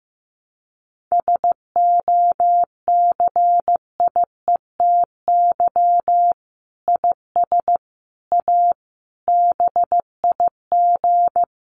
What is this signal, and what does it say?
Morse code, heard as a keyed tone, about 15 wpm, 700 Hz. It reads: SOCIETY IS A BIG
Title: Sentences from Top 1500 Words 15wpm